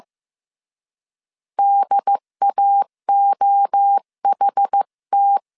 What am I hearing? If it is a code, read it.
Morse code, heard as a keyed tone, about 15 wpm, 800 Hz. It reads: DAOHT
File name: CW / DET / 1 3 1500C S A 14wpm 2447